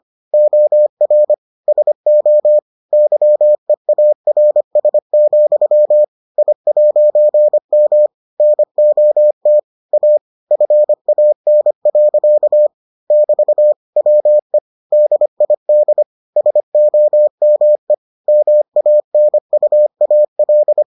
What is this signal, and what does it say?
Morse code, heard as a keyed tone, about 25 wpm, 600 Hz. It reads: OR SO YEARS, I'M NOT A FAN. <BT> WE DID SOME MANUAL